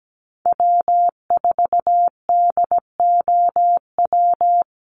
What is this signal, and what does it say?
Morse code, heard as a keyed tone, about 17 wpm, 700 Hz. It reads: W4DOW
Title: NA SOTA Chaser Callsigns 17wpm